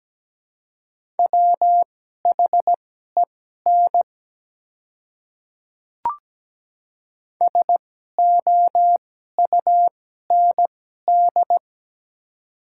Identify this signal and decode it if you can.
Morse code, heard as a keyed tone, about 17 wpm, spaced as for 12 wpm, 700 Hz. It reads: WHEN SOUND
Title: 2x Character Spacing - Top 100 Words - 17wpm